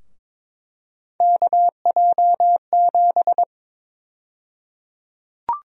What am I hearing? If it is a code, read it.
Morse code, heard as a keyed tone, about 22 wpm, 700 Hz. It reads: KJ7
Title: USA Call Sign Prefixes - 22wpm